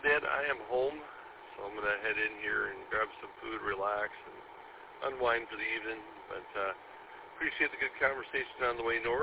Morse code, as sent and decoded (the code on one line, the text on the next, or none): none